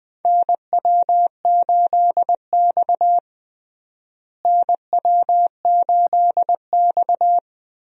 Morse code, heard as a keyed tone, about 21 wpm, 700 Hz. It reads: NW8X NW8X
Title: USA Call Signs - 4 Characters - Repeated 3x - 20wpm